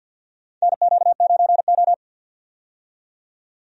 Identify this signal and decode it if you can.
Morse code, heard as a keyed tone, about 48 wpm, 700 Hz. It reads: NQ9O
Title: USA Call Signs - 4 Characters - 50wpm